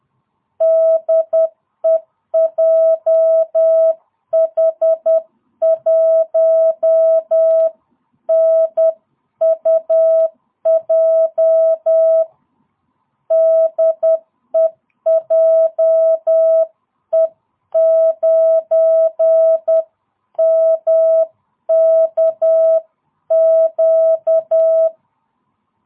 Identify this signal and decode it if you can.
Morse code, heard as a keyed tone, about 10 wpm, 660 Hz. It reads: DE JH1NUJ DE JE9MKQ